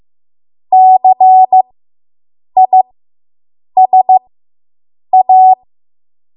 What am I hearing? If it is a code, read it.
Morse code, heard as a keyed tone, about 15 wpm, spaced as for 7 wpm, 750 Hz. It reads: CISA